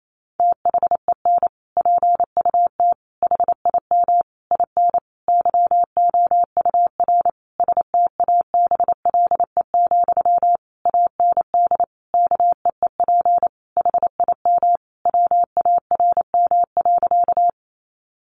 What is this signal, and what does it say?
Morse code, heard as a keyed tone, about 28 wpm, 700 Hz. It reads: THEN PUT HIM IN YOUR STABLE, AND KEEP HIM WARM.
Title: Sentences from Top 1500 Words 28wpm